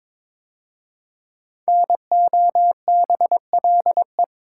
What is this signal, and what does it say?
Morse code, heard as a keyed tone, about 22 wpm, 700 Hz. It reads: NOBLE
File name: 5 Letter Words - First 3 Letters Given - 22wpm